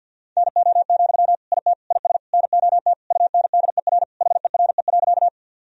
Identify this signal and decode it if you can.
Morse code, heard as a keyed tone, about 52 wpm, 700 Hz. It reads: NO, IT IS NOT UNDER HERE.
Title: Sentences from Top 1000 Words - Review Part 1 - 50wpm